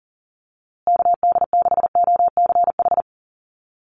Morse code, heard as a keyed tone, about 40 wpm, 700 Hz. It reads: KD6OCH
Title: USA Call Signs Repeated 3x 40wpm